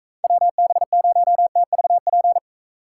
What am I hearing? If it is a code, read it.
Morse code, heard as a keyed tone, about 45 wpm, 700 Hz. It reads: WB0TVP